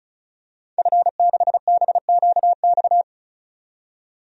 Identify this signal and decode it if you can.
Morse code, heard as a keyed tone, about 34 wpm, 700 Hz. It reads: F6BQX
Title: DXCC-100 Call Signs - 5 Characters - 35wpm